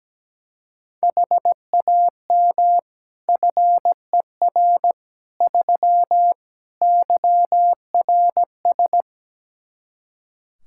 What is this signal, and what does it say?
Morse code, heard as a keyed tone, about 17 wpm, 700 Hz. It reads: HAM FER 3 YRS
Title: Practice QSO - Exchange 3 - More Info - 17wpm